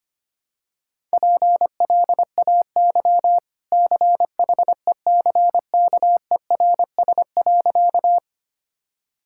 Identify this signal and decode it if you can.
Morse code, heard as a keyed tone, about 25 wpm, 700 Hz. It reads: PLAY CHECKERS.
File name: Easy Sentences with Hints - 25wpm